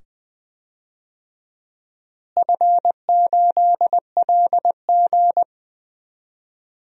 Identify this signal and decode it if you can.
Morse code, heard as a keyed tone, about 20 wpm, 700 Hz. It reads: F8LG
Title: DXCC-100 Call Signs - 4 Characters - Repeated 2x - 20wpm